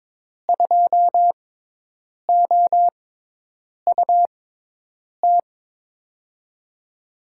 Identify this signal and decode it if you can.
Morse code, heard as a keyed tone, about 23 wpm, spaced as for 8 wpm, 700 Hz. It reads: 2OUT
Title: USA Call Signs - 6 Characters - 6x Character Spacing - 22wpm